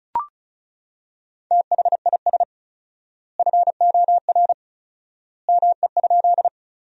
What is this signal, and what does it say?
Morse code, heard as a keyed tone, about 34 wpm, 700 Hz. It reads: THIS FOR ME?
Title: 4x Word-Spacing - Sentences from Top 200 Words - 35wpm